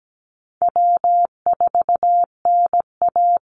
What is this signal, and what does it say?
Morse code, heard as a keyed tone, about 17 wpm, 700 Hz. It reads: W4NA
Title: Easy Call Signs - 17wpm